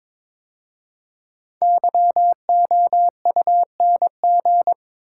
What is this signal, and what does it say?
Morse code, heard as a keyed tone, about 22 wpm, 700 Hz. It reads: YOUNG
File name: 5 Letter Words - First 3 Letters Given - 22wpm